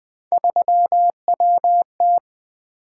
Morse code, heard as a keyed tone, about 20 wpm, 700 Hz. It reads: 3WT
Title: Three Letter-Number Combinations - 20wpm